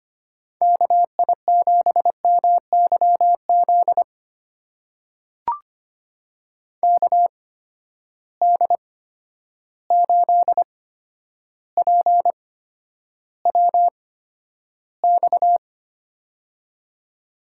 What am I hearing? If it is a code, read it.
Morse code, heard as a keyed tone, about 25 wpm, spaced as for 11 wpm, 700 Hz. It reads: KI7MYZ KD8PWX
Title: USA Call Signs - 8x Character Spacing - 25wpm